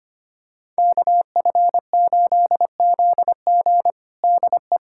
Morse code, heard as a keyed tone, about 25 wpm, 700 Hz. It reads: KF8ZG DE
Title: USA Callsign DE Callsign - 5 Characters - 25wpm